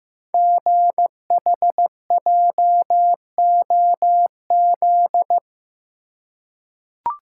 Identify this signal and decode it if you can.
Morse code, heard as a keyed tone, about 15 wpm, 700 Hz. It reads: GHJOZ
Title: Code Groups - 5 Letters - 15wpm